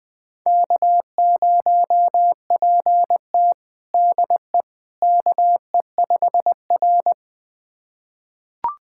Morse code, heard as a keyed tone, about 20 wpm, 700 Hz. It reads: K0PT DE KE5R
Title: USA Callsign DE Callsign - 4 Characters - 20wpm